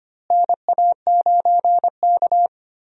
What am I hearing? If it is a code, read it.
Morse code, heard as a keyed tone, about 25 wpm, 700 Hz. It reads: NA9K
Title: USA Call Signs - 4 Characters - Repeated 2x - 25wpm